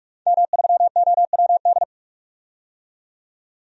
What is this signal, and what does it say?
Morse code, heard as a keyed tone, about 46 wpm, 700 Hz. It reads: M3OWD
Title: DXCC-100 Call Signs - 5 Characters - 45wpm